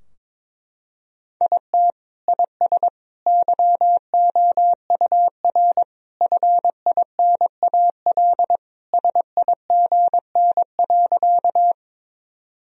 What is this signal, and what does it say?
Morse code, heard as a keyed tone, about 22 wpm, 700 Hz. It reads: IT IS YOUR FINAL SIGN.
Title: Sentences from Top 1000 Words - Review Part 2 - 22wpm